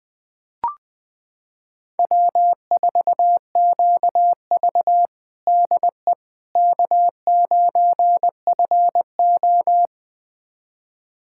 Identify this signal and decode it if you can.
Morse code, heard as a keyed tone, about 21 wpm, 700 Hz. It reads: W4QV DE K9FO
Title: USA Callsign DE Callsign - 4 Characters - 20wpm